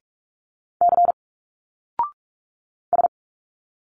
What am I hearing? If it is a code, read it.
Morse code, heard as a keyed tone, about 46 wpm, 700 Hz. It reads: C S
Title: Single Letters 45wpm